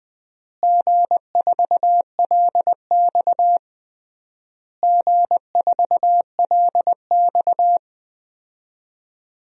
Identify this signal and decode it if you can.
Morse code, heard as a keyed tone, about 20 wpm, 700 Hz. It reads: G4LX G4LX
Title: DXCC-100 Call Signs - 4 Characters - Repeated 3x - 20wpm